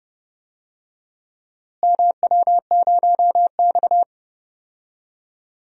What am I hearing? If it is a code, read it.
Morse code, heard as a keyed tone, about 31 wpm, 700 Hz. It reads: MW0X
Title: DXCC-100 Call Signs - 4 Characters - Repeated 2x - 30wpm